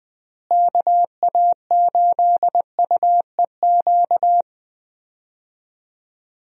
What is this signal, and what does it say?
Morse code, heard as a keyed tone, about 20 wpm, 700 Hz. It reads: KA8UEQ
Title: USA Call Signs - 6 Characters - Repeated 3x - 20wpm